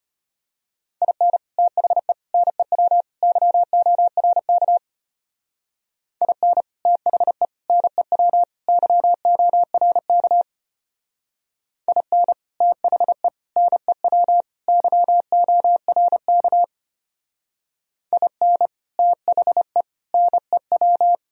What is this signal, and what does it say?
Morse code, heard as a keyed tone, about 30 wpm, 700 Hz. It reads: IN THE NEW YORK IN THE NEW YORK IN THE NEW YORK IN THE NEW
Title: Speed-Racing - Sets of 4 Words 25wpm